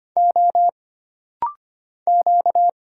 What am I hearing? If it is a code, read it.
Morse code, heard as a keyed tone, about 25 wpm, 700 Hz. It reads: O Q
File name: Single Character (Letter:Number:Punctuation:Prosign) - Rapid-Fire - 25wpm